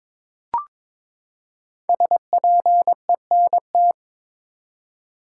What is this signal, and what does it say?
Morse code, heard as a keyed tone, about 21 wpm, 700 Hz. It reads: SPENT
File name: practice / 22wpm / Build A Phrase - 5-Word Phrases - 22wpm